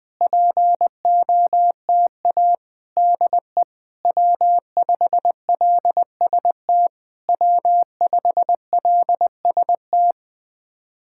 Morse code, heard as a keyed tone, about 20 wpm, 700 Hz. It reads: POTA DE W5LST W5LST
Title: POTA QSO Practice - 20wpm